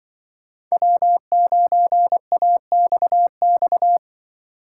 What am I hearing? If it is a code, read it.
Morse code, heard as a keyed tone, about 24 wpm, 700 Hz. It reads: W9AXX
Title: Speed Racing - USA Call Signs 20wpm